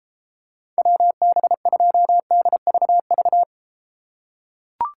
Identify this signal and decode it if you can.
Morse code, heard as a keyed tone, about 31 wpm, 700 Hz. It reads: WB2DVV